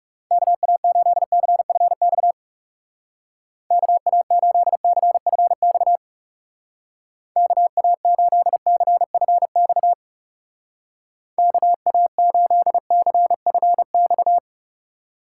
Morse code, heard as a keyed tone, about 35 wpm, 700 Hz. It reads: KA8CFX KA8CFX KA8CFX KA8CFX
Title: Speed Racing - USA Call Signs 30wpm